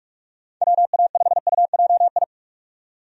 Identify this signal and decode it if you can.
Morse code, heard as a keyed tone, about 46 wpm, 700 Hz. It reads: WA5UJI
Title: USA Call Signs - 6 Characters - 45wpm